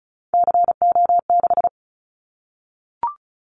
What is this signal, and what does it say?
Morse code, heard as a keyed tone, about 34 wpm, 700 Hz. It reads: CO6
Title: Callsign Prefixes for Top 100 DXCC - 35wpm